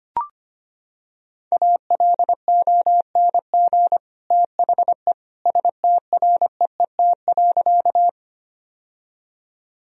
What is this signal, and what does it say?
Morse code, heard as a keyed tone, about 25 wpm, 700 Hz. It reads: ALONG THE STREET.
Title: Sentences from Top 1000 Words - Review Part 3 - 25wpm